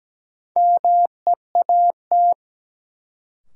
Morse code, heard as a keyed tone, about 17 wpm, 700 Hz. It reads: MEAT
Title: Sentences from Top 1200 Words - New Word Drill Alt - 17wpm